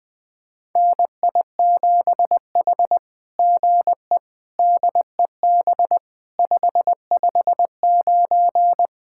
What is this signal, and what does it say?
Morse code, heard as a keyed tone, about 20 wpm, 700 Hz. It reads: NI7H GE DEB 559
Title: SOTA QSO Practice 20wpm